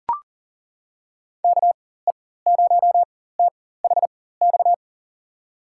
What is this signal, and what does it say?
Morse code, heard as a keyed tone, about 40 wpm, spaced as for 19 wpm, 700 Hz. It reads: KE0THX
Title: USA Call Signs - 6 Characters - 4x Character Spacing - 40wpm